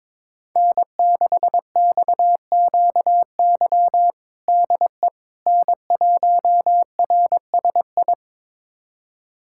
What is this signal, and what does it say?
Morse code, heard as a keyed tone, about 22 wpm, 700 Hz. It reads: N6XQY DE N1RSI
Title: USA Callsign DE Callsign - 5 Characters - 22wpm